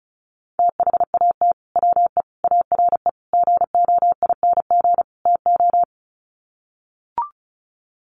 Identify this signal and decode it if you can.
Morse code, heard as a keyed tone, about 34 wpm, 700 Hz. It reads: THAT WE ARE GOING TO
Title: Sets of 5 Words Encore 35wpm